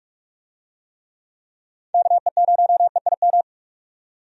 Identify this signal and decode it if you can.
Morse code, heard as a keyed tone, about 46 wpm, 700 Hz. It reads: KE0EIM
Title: USA Call Signs - 6 Characters - 45wpm